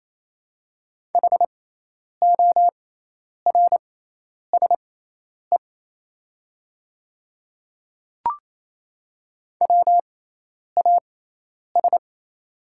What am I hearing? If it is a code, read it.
Morse code, heard as a keyed tone, about 28 wpm, spaced as for 10 wpm, 700 Hz. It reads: HORSE WAS
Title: 6x Character Spacing - Top 300 Words - 28wpm